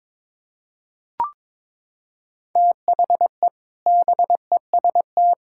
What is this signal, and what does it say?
Morse code, heard as a keyed tone, about 22 wpm, 700 Hz. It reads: THE BEST
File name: practice / 22wpm / Sentences with 4-Word Phrases - 22wpm